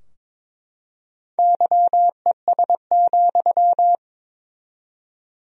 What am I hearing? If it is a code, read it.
Morse code, heard as a keyed tone, about 23 wpm, 700 Hz. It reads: YES,
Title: 4x Word-Spacing - Sentences from Top 500 Words - 22wpm